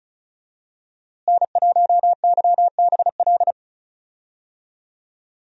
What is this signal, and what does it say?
Morse code, heard as a keyed tone, about 36 wpm, 700 Hz. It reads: N1YBL
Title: USA Call Signs - 5 Characters - Repeated 2x - 35wpm